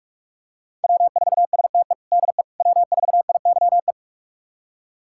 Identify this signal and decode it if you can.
Morse code, heard as a keyed tone, about 42 wpm, 700 Hz. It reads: W4STE DE W4IYE
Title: USA Callsign DE Callsign - 5 Characters - 45wpm